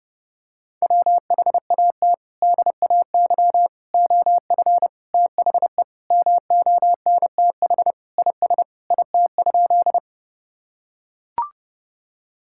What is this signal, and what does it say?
Morse code, heard as a keyed tone, about 32 wpm, 700 Hz. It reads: WHAT DAY OF THE MONTH IS IT?
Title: Sentences from Top 1000 Words - Review Part 5 - 30wpm